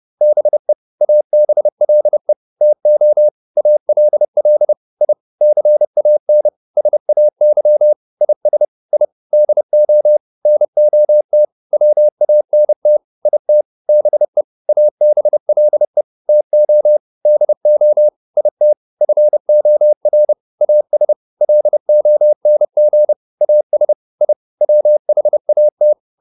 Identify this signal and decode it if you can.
Morse code, heard as a keyed tone, about 30 wpm, 600 Hz. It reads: BE ABLE TO ALL I CAN SAY IS I DO NOT WANT IT BE ABLE TO DO IT FOR AS LONG AS I WHAT